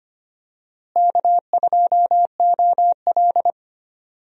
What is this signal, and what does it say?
Morse code, heard as a keyed tone, about 25 wpm, 700 Hz. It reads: K2OL
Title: USA Call Signs - 4 Characters - Repeated 3x - 25wpm